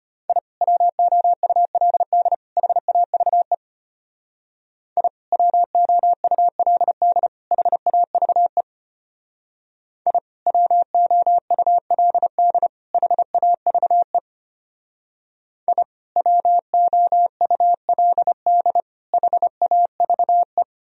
Morse code, heard as a keyed tone, about 30 wpm, 700 Hz. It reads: I WOULD HAVE I WOULD HAVE I WOULD HAVE I WOULD HAVE
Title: Speed-Racing - Sets of 3 Words 25wpm